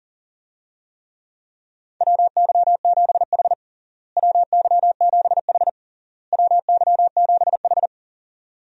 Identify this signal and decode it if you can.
Morse code, heard as a keyed tone, about 40 wpm, 700 Hz. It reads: WY7H WY7H WY7H
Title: USA Call Signs - 4 Characters - Repeated 3x - 40wpm